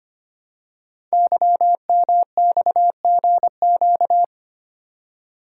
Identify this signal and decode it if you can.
Morse code, heard as a keyed tone, about 25 wpm, 700 Hz. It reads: YMXGQ
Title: Code Groups - 5 Letters - 25wpm